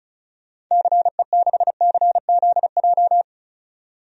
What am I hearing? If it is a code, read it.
Morse code, heard as a keyed tone, about 36 wpm, 700 Hz. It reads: CE6CZJ
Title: DXCC-100 Call Signs - 6 Characters - 35wpm